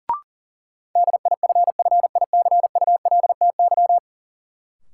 Morse code, heard as a keyed tone, about 37 wpm, 700 Hz. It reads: DIFFICULTY
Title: Sentences from Top 1500 Words - New Word Drill - 40wpm